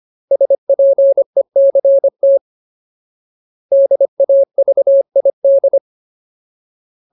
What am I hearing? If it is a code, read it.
Morse code, heard as a keyed tone, about 25 wpm, 550 Hz. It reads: SPECT DAVID
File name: Ultimate Word List / 25WPM Ultimate